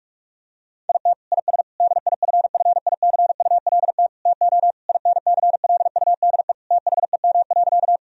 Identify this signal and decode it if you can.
Morse code, heard as a keyed tone, about 47 wpm, 700 Hz. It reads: IT IS DIFFICULT TO INCLUDE THEM.
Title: Sentences from Top 1000 Words - Review Part 3 - 45wpm